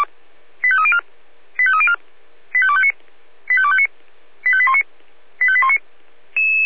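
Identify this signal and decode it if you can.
Five-tone selective call (ZVEI), heard as quick runs of five tones, tones of about 70 ms. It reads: 85304 85304 85259 85259 85819 85819
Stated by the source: ZVEI five-tone call 85250 received by SDR, on a server of the Feuerwehr Orlamünde fire brigade.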